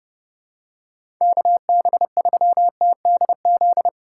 Morse code, heard as a keyed tone, about 30 wpm, 700 Hz. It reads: KB3TDZ
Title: USA Call Signs - 6 Characters - 8x Character Spacing - 30wpm